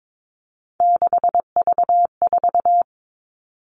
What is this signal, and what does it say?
Morse code, heard as a keyed tone, about 22 wpm, 700 Hz. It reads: 6V4
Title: All Cut Numbers - 3-Digit 22wpm